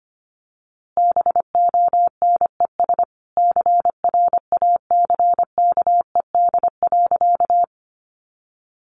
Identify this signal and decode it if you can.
Morse code, heard as a keyed tone, about 25 wpm, 700 Hz. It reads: BONES CRACKED.
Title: A Sound Of Thunder 25wpm